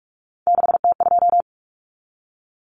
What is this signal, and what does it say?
Morse code, heard as a keyed tone, about 44 wpm, 700 Hz. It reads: 6T2
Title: Common Cut Numbers - 3-Digit 45wpm